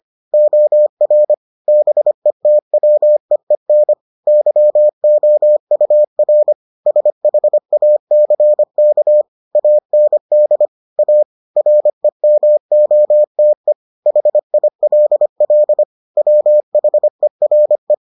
Morse code, heard as a keyed tone, about 25 wpm, 600 Hz. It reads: OR BETWEEN YOUR SHACK AND A REMOTE HILL WHERE